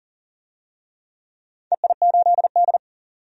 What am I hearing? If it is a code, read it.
Morse code, heard as a keyed tone, about 40 wpm, 700 Hz. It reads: EI8D
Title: DXCC-100 Call Signs - 4 Characters - 40wpm